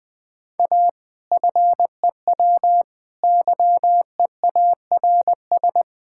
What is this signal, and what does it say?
Morse code, heard as a keyed tone, about 20 wpm, 700 Hz. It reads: A FEW YEARS
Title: Build A Phrase - 4-Word Phrases - 20wpm